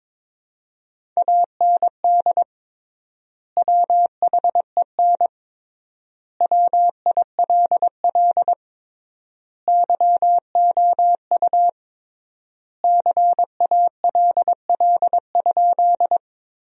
Morse code, heard as a keyed tone, about 22 wpm, 700 Hz. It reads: AND WHEN WILL YOU CALL?
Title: 3x Word-Spacing - Sentences from Top 100 Words - 22wpm